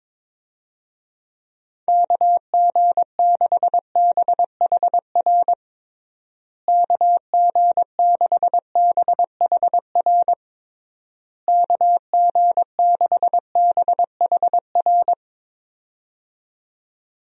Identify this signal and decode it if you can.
Morse code, heard as a keyed tone, about 22 wpm, 700 Hz. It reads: KG6BHR KG6BHR KG6BHR
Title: USA Call Signs - 6 Characters - Repeated 3x - 22wpm